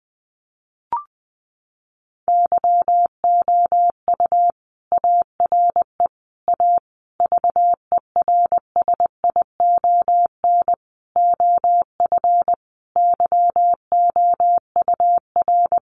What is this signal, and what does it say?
Morse code, heard as a keyed tone, about 19 wpm, 700 Hz. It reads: YOU ARE A VERSION OF YOUR
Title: Sentences from Top 1400 Words 20wpm